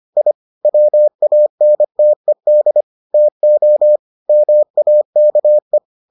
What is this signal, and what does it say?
Morse code, heard as a keyed tone, about 25 wpm, 600 Hz. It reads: I WANTED TO MAKE